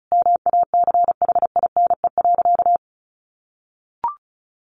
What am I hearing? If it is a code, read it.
Morse code, heard as a keyed tone, about 34 wpm, 700 Hz. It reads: MACHINE.
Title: A Sound Of Thunder 35wpm